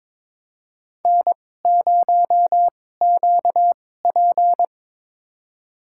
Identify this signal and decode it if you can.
Morse code, heard as a keyed tone, about 22 wpm, spaced as for 16 wpm, 700 Hz. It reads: N0QP
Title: USA Call Signs - 4 Characters - 2x Character Spacing - 22wpm